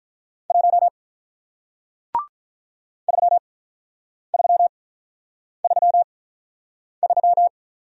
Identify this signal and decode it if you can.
Morse code, heard as a keyed tone, about 48 wpm, spaced as for 9 wpm, 700 Hz. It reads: 1 3333